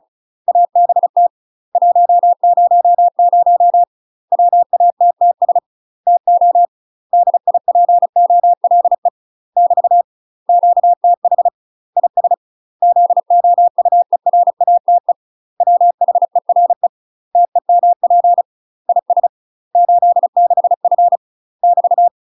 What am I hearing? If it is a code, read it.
Morse code, heard as a keyed tone, about 34 wpm, 700 Hz. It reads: ABT 100 WATTS TO DIPOLE <BT> QTH IS ZOUERATE WHERE TEMP IS 86F <BT>
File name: UK MASTER QSOs TRAINING/35WPM